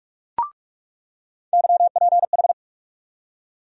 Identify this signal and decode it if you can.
Morse code, heard as a keyed tone, about 43 wpm, 700 Hz. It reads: YPH
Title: USA Call Sign Postfixes - 3 Letters 45wpm